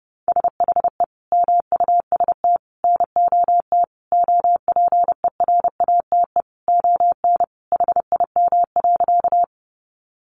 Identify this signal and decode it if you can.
Morse code, heard as a keyed tone, about 30 wpm, 700 Hz. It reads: SHE MUST NOT OPERATE ON HIM.
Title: Sentences from Top 1500 Words 30wpm